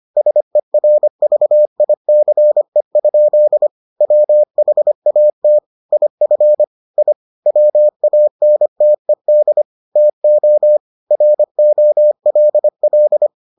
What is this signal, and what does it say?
Morse code, heard as a keyed tone, about 25 wpm, 600 Hz. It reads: SERVICE? WHAT IF I WANTED TO ROLL